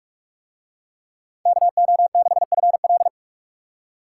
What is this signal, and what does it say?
Morse code, heard as a keyed tone, about 42 wpm, 700 Hz. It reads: KO6FL